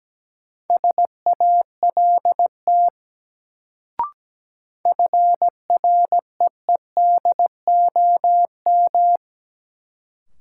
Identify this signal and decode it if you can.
Morse code, heard as a keyed tone, about 17 wpm, 700 Hz. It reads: SALT FREEDOM
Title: Sentences from Top 800 Words - New Word Drill - 17wpm